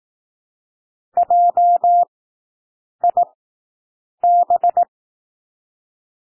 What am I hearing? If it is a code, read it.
Morse code, heard as a keyed tone, about 18 wpm, spaced as for 7 wpm, 700 Hz. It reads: JIB
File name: SOUND WORDS 18 wpm